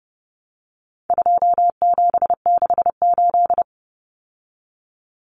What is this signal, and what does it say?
Morse code, heard as a keyed tone, about 29 wpm, 700 Hz. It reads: 2768
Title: Common Cut Numbers - 4-Digit 30wpm